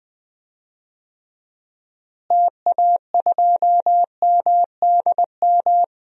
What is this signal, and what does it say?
Morse code, heard as a keyed tone, about 21 wpm, 690 Hz. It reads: TA2MDM
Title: DXCC-100 Call Signs - 6 Characters - Repeated 2x - 20wpm